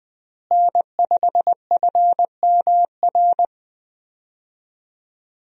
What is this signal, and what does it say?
Morse code, heard as a keyed tone, about 20 wpm, 700 Hz. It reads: N5FMR
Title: USA Call Signs - 5 Characters - Repeated 3x - 20wpm